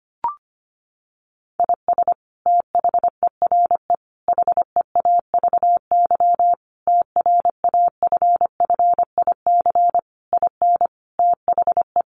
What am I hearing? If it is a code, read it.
Morse code, heard as a keyed tone, about 25 wpm, 700 Hz. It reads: IS THERE HEAVY TRAFFIC IN THE
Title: Sentences from Top 1300 Words 25wpm